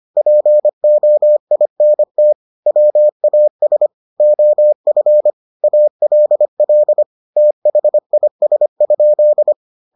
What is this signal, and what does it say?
Morse code, heard as a keyed tone, about 25 wpm, 600 Hz. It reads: POINT WAS OF ALL THIS?